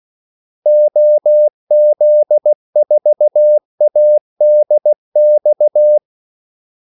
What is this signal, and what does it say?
Morse code, heard as a keyed tone, about 16 wpm, 600 Hz. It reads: OZ4ADX
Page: Kaldesignaler 16 wpm